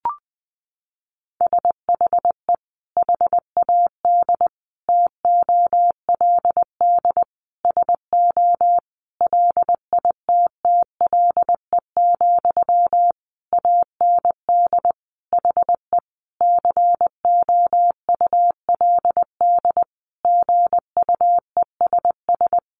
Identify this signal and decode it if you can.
Morse code, heard as a keyed tone, about 20 wpm, 700 Hz. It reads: SHE HAD TOLD SO LITTLE, AND HE COULD GUESS
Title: Sentences from Top 1500 Words 20wpm